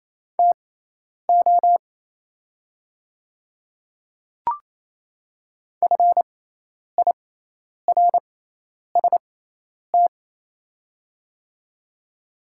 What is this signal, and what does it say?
Morse code, heard as a keyed tone, about 28 wpm, spaced as for 10 wpm, 700 Hz. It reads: TO FIRST
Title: 6x Character Spacing - Top 100 Words - 28wpm